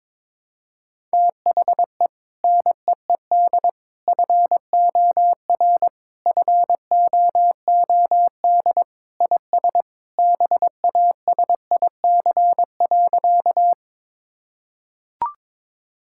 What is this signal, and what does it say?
Morse code, heard as a keyed tone, about 22 wpm, 700 Hz. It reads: THE NEED FOR FOOD IS BASIC.
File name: practice / 22wpm / Sentences with 3-Word Phrases - 22wpm